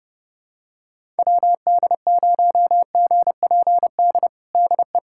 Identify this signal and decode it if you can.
Morse code, heard as a keyed tone, about 32 wpm, 700 Hz. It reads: WD0GPD DE
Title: USA Callsign DE Callsign 30wpm